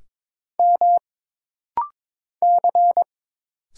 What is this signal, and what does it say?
Morse code, heard as a keyed tone, about 22 wpm, 700 Hz. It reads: M C
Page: This rapid-fire timing is geared to push your ICR (Instant Character Recognition) skills.